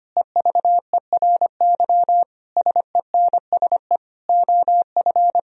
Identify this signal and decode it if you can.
Morse code, heard as a keyed tone, about 25 wpm, 700 Hz. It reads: EVERY SENSE OF
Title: Sentences from Top 1000 Words - Review Part 4 - 25wpm